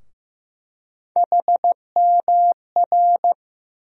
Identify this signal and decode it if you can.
Morse code, heard as a keyed tone, about 15 wpm, 700 Hz. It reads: HMR
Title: USA Call Sign Postfixes - 3 Letters 15wpm